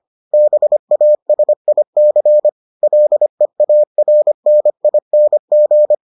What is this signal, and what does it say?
Morse code, heard as a keyed tone, about 25 wpm, 600 Hz. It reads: BASIC LEARNING